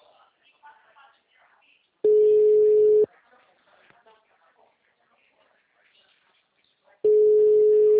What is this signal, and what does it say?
Ringback tone (Europe, 425 Hz), with speech.
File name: out- -2008- - - .51172